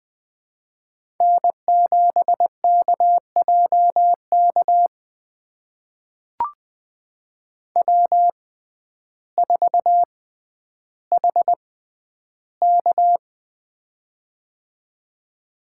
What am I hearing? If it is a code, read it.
Morse code, heard as a keyed tone, about 20 wpm, spaced as for 11 wpm, 700 Hz. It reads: N7KJK W4HK
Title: USA Call Signs - 6x Character Spacing - 20wpm